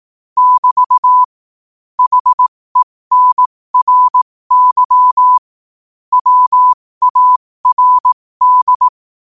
Morse code, heard as a keyed tone, about 18 wpm, spaced as for 15 wpm, 1000 Hz. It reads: <BT> HENRY WARD
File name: qotdfiles / 15wpm